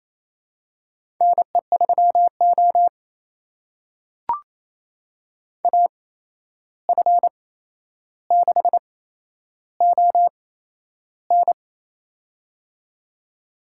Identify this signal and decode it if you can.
Morse code, heard as a keyed tone, about 28 wpm, spaced as for 11 wpm, 700 Hz. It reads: NE3O AF6ON
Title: USA Call Signs - 8x Character Spacing - 28wpm